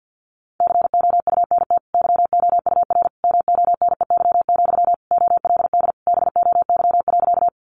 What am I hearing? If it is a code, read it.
Morse code, heard as a keyed tone, about 50 wpm, 700 Hz. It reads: COUNT YOUR MONEY, OLD BOY.
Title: Sentences from Top 1500 Words 50wpm